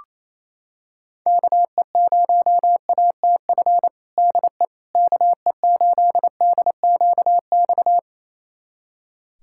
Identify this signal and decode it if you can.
Morse code, heard as a keyed tone, about 28 wpm, 700 Hz. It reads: KE0ATF DE KE8DQX
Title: USA Callsign DE Callsign 28wpm